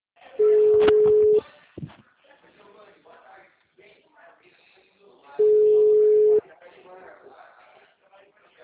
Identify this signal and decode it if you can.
Ringback tone (Europe, 425 Hz), with speech.